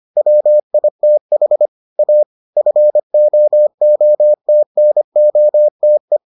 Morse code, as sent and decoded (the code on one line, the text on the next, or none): .-- .. - .... / .- / ..-. --- --- - -. --- - .
WITH A FOOTNOTE